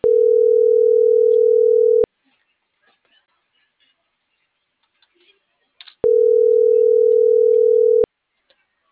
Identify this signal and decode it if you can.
Ringback tone (North America), 440 + 480 Hz.
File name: CENTRAL NMP / out melanie.diaznmp - .42082